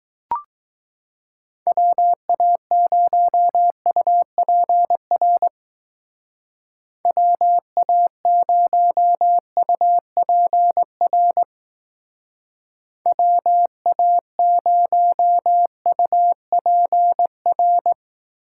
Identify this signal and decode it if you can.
Morse code, heard as a keyed tone, about 19 wpm, 700 Hz. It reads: WA0UPR WA0UPR WA0UPR
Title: Speed Racing - USA Call Signs 15wpm